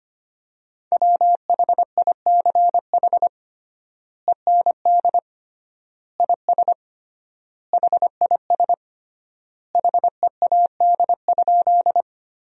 Morse code, heard as a keyed tone, about 25 wpm, 700 Hz. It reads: WHICH END IS HIS HEAD?
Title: 3x Word-Spacing - Sentences from Top 300 Words - 25wpm